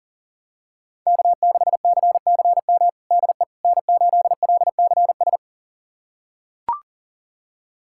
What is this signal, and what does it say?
Morse code, heard as a keyed tone, about 40 wpm, 700 Hz. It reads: K6CCM DE N8LCS
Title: USA Callsign DE Callsign - 5 Characters - 40wpm